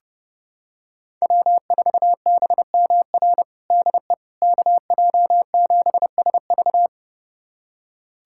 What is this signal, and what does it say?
Morse code, heard as a keyed tone, about 30 wpm, 700 Hz. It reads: W4BMR DE KJ7SV
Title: USA Callsign DE Callsign - 5 Characters - 30wpm